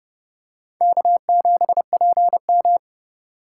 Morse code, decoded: K7PM

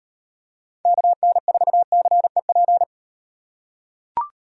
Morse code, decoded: KN4CEP

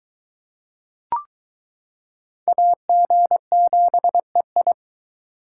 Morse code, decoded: AG7EI

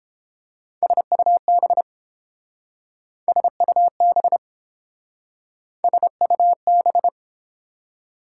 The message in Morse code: SUB SUB SUB

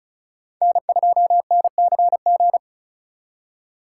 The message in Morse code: N2NCG

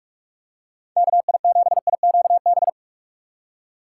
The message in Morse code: KI7IQB

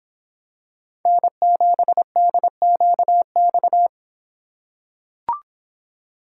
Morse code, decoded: N7DQX

